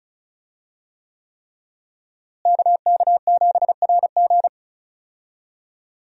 Morse code, heard as a keyed tone, about 34 wpm, 700 Hz. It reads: KK7RG
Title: USA Call Signs - 5 Characters - Repeated 3x - 35wpm